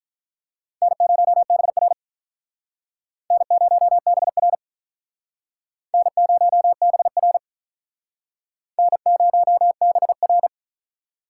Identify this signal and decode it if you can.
Morse code, heard as a keyed tone, about 47 wpm, 700 Hz. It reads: N0BR N0BR N0BR N0BR